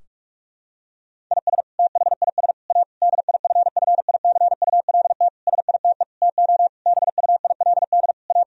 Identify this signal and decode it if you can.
Morse code, heard as a keyed tone, about 46 wpm, 700 Hz. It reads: IS THIS A DIFFICULT SITE TO BUILD A
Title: Sentences from Top 1000 Words - Review Part 4 - 45wpm